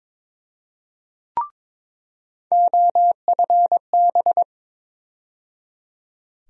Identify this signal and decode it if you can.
Morse code, heard as a keyed tone, about 22 wpm, 700 Hz. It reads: OFB
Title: USA Call Sign Postfixes - 3 Letters 22wpm